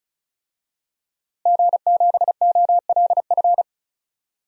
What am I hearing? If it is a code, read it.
Morse code, heard as a keyed tone, about 34 wpm, 700 Hz. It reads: G7OLF
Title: DXCC-100 Call Signs - 5 Characters - Repeated 2x - 35wpm